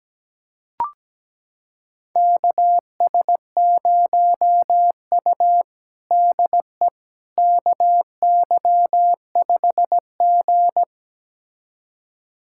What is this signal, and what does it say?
Morse code, heard as a keyed tone, about 17 wpm, 700 Hz. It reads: KS0U DE KY5G
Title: USA Callsign DE Callsign - 4 Characters - 17wpm